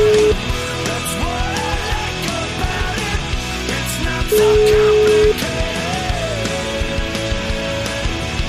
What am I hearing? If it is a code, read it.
Ringback tone (Europe, 425 Hz), with noise.